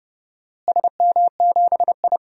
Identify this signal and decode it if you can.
Morse code, heard as a keyed tone, about 31 wpm, 700 Hz. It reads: SM7I
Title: DXCC-100 Call Signs - 4 Characters - Repeated 2x - 30wpm